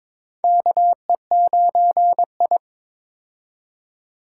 Morse code, heard as a keyed tone, about 22 wpm, 700 Hz. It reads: KE9I
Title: K1USN SST QSO Practice - 22wpm